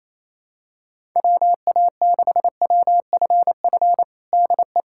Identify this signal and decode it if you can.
Morse code, heard as a keyed tone, about 28 wpm, 700 Hz. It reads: WA6WFF DE
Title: USA Callsign DE Callsign 28wpm